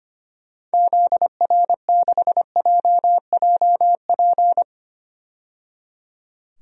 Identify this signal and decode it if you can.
Morse code, heard as a keyed tone, about 25 wpm, 700 Hz. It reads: ZR6JJP
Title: DXCC-100 Call Signs - 6 Characters - 25wpm